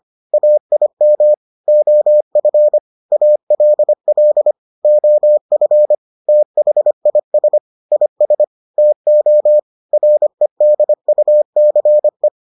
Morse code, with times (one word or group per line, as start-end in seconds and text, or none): 0.3-1.3 AIM
1.7-2.8 OF
3.1-4.5 ALL
4.8-5.9 OF
6.3-7.6 THIS
7.9-8.4 IS
8.8-9.6 TO
9.9-12.3 REDUCE